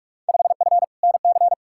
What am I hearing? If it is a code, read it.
Morse code, heard as a keyed tone, about 46 wpm, 700 Hz. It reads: 5F NC
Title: Field Day Exchanges Encore - 45wpm